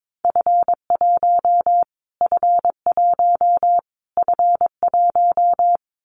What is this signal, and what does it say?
Morse code, heard as a keyed tone, about 22 wpm, 700 Hz. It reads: F1 F1 F1
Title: Callsign Prefixes for Top 25 DXCC - Repeated 3x - 22wpm